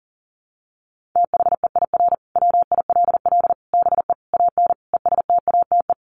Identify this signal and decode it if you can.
Morse code, heard as a keyed tone, about 40 wpm, 700 Hz. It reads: THEIR WILL BE AN ESTATE
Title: Sentences from Top 1500 Words 40wpm